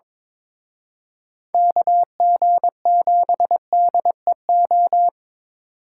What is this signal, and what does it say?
Morse code, heard as a keyed tone, about 25 wpm, 700 Hz. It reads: KG7DEO
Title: Speed Racing - USA 2x3 Call Signs 22wpm